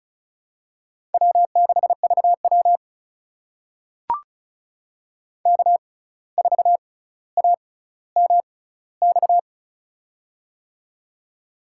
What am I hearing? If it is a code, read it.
Morse code, heard as a keyed tone, about 36 wpm, spaced as for 17 wpm, 700 Hz. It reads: W6VW K4AMX
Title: USA Call Signs - 6x Character Spacing - 35wpm